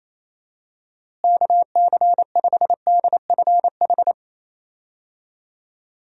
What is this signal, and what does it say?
Morse code, heard as a keyed tone, about 28 wpm, 700 Hz. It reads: KC5DFH
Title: USA Call Signs - 6 Characters - Repeated 3x - 28wpm